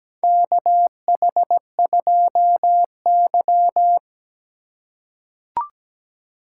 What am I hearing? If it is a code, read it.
Morse code, heard as a keyed tone, about 17 wpm, 700 Hz. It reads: KH2Y
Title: USA Call Signs - 4 Characters - 3x Character Spacing - 17wpm